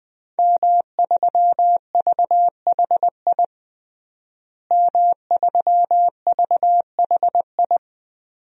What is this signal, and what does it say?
Morse code, heard as a keyed tone, about 20 wpm, 700 Hz. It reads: M3VHI M3VHI
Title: DXCC-100 Call Signs - 5 Characters - Repeated 2x - 20wpm